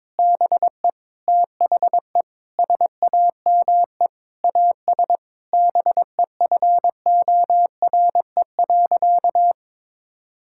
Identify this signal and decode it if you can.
Morse code, heard as a keyed tone, about 21 wpm, 700 Hz. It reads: BE THE SAME AS BEFORE.
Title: Sentences from Top 1000 Words - Review Part 5 - 22wpm